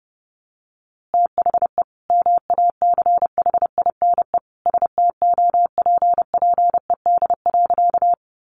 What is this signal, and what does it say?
Morse code, heard as a keyed tone, about 30 wpm, 700 Hz. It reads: THE MACHINE STOPPED.
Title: A Sound Of Thunder 30wpm